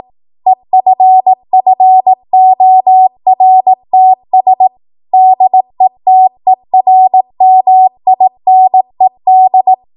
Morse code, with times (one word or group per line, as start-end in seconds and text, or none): 0.5-4.7 EFFORTS
5.1-9.7 DETERMINED